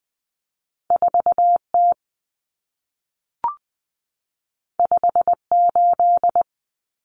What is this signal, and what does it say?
Morse code, heard as a keyed tone, about 20 wpm, 700 Hz. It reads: 4T 58
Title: Common Cut Numbers - 2-Digit 20wpm